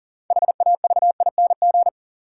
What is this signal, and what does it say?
Morse code, heard as a keyed tone, about 39 wpm, 700 Hz. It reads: HAVING
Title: Sets of 5 Words Encore - New Word Drill - 40wpm